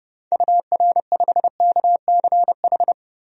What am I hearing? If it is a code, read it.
Morse code, heard as a keyed tone, about 32 wpm, 700 Hz. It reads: UR5KCH
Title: DXCC-100 Call Signs - 6 Characters - 30wpm